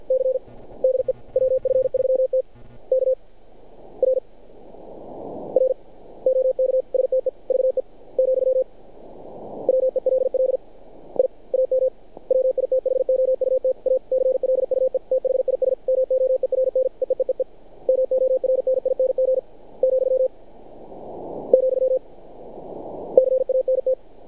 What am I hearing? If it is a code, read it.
Morse code, heard as a keyed tone, about 46 wpm, 530 Hz. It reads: C DE WF3T K R R OK STEVE , WELL I AM WITHOUT A CLUE THIS MOERN EEEEE MORNING , , CANT
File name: archived / CRSnet